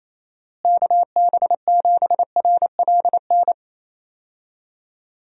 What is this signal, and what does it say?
Morse code, heard as a keyed tone, about 28 wpm, 700 Hz. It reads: KB7RLN
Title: USA Call Signs - 6 Characters - Repeated 2x - 28wpm